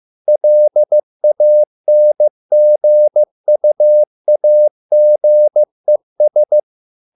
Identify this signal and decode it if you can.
Morse code, heard as a keyed tone, about 15 wpm, 600 Hz. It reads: LANGUAGES